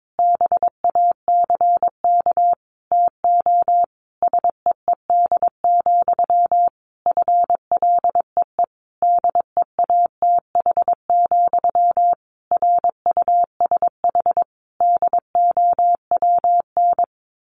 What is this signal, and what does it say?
Morse code, heard as a keyed tone, about 22 wpm, 700 Hz. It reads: BACK TO SEED, FLEE DEATH, RUSH DOWN
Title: A Sound Of Thunder 22wpm